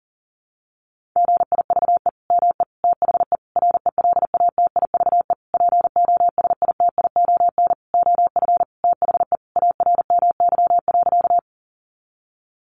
Morse code, heard as a keyed tone, about 40 wpm, 700 Hz. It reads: GIVE ME THE RELATIVE POSITION OF THE ARMY.
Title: Sentences from Top 1400 Words 40wpm